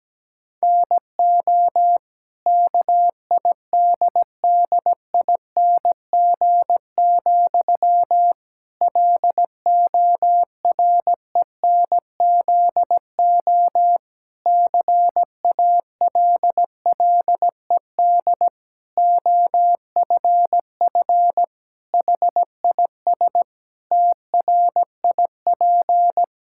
Morse code, heard as a keyed tone, about 17 wpm, 700 Hz. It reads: NO KIDDING, LORENZO CALLED OFF HIS TRIP
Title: Pangram Challenge v2 - 17wpm